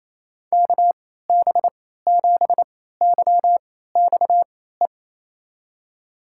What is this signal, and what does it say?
Morse code, heard as a keyed tone, about 28 wpm, spaced as for 16 wpm, 700 Hz. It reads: KB7YXE
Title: USA Call Signs - 6 Characters - 3x Character Spacing - 28wpm